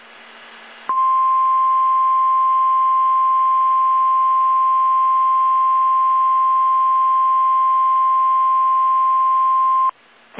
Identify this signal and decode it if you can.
Emergency Alert System (SAME) tones (the 1050 Hz attention tone). It is heard from 1.0 to 9.9 s.